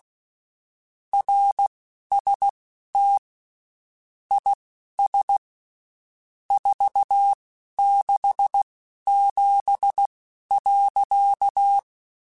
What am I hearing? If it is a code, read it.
Morse code, heard as a keyed tone, about 16 wpm, spaced as for 11 wpm, 800 Hz. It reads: RST IS 467.
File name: (spaced) / QSOs Spaced-mp3s / kc7haa @15 sp